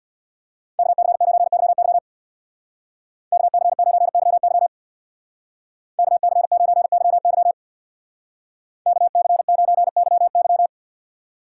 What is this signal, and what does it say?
Morse code, heard as a keyed tone, about 64 wpm, 690 Hz. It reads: KC9YY KC9YY KC9YY KC9YY